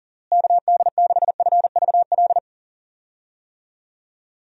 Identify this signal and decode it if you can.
Morse code, heard as a keyed tone, about 40 wpm, 700 Hz. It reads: KD6FVL
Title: USA Call Signs - 6 Characters - Repeated 2x - 40wpm